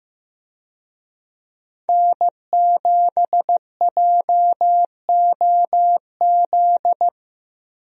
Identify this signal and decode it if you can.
Morse code, heard as a keyed tone, about 15 wpm, 700 Hz. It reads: N7JOZ
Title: USA Call Signs - 5 Characters - Repeated 2x - 15wpm